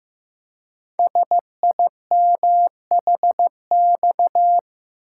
Morse code, heard as a keyed tone, about 15 wpm, 700 Hz. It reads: SIMHX
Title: Code Groups - 5 Letters - 15wpm